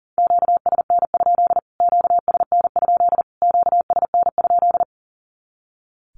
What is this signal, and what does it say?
Morse code, heard as a keyed tone, about 40 wpm, 700 Hz. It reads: QSN? QSN? QSN?
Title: Q-Codes Repeated 3x 40wpm